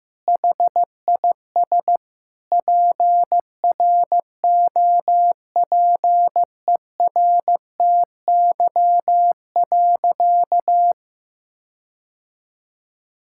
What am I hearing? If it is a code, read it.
Morse code, heard as a keyed tone, about 15 wpm, 700 Hz. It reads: HIS PROPERTY.
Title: Sentences from Top 1600 Words 15wpm